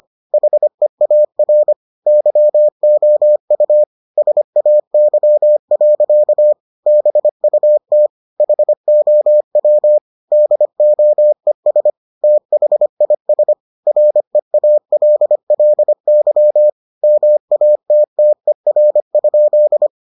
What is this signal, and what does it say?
Morse code, heard as a keyed tone, about 25 wpm, 600 Hz. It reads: HEAR YOU SAY. BUT HOW DOES THIS REALLY MATTER?